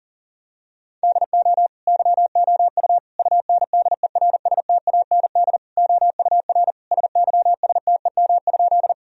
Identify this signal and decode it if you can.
Morse code, heard as a keyed tone, about 37 wpm, 700 Hz. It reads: DO YOU UNDERSTAND OUR SYSTEM?
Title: Sentences from Top 1000 Words - Review Part 3 - 40wpm